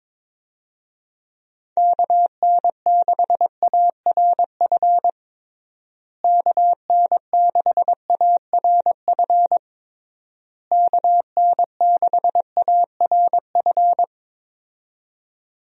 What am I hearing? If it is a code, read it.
Morse code, heard as a keyed tone, about 22 wpm, 700 Hz. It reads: KN6ARF KN6ARF KN6ARF
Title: USA Call Signs - 6 Characters - Repeated 3x - 22wpm